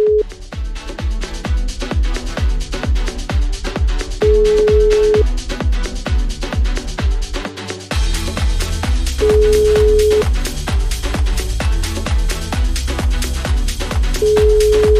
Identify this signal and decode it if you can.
Ringback tone (Europe, 425 Hz).